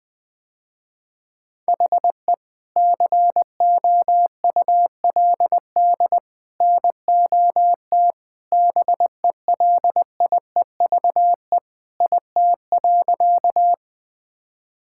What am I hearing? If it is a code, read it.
Morse code, heard as a keyed tone, about 20 wpm, 700 Hz. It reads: HE COULD NOT BELIEVE IT.
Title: Sentences with 3-Word Phrases - Spoken Beginning - 20wpm